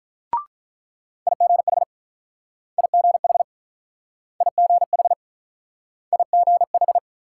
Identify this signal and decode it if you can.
Morse code, heard as a keyed tone, about 44 wpm, 700 Hz. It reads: IGH IGH IGH IGH
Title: Speed-Racing Multi-letter Phonemes - 35wpm